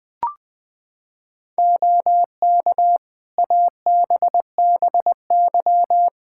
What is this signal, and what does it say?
Morse code, heard as a keyed tone, about 20 wpm, 700 Hz. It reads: OK ABBY
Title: Practice QSO - Exchange 3 - More Info - 20wpm